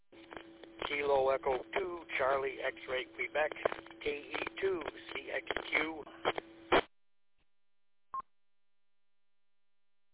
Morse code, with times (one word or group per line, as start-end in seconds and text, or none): none